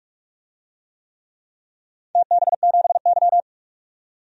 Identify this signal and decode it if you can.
Morse code, heard as a keyed tone, about 44 wpm, 690 Hz. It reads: TB7Y
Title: DXCC-100 Call Signs - 4 Characters - 45wpm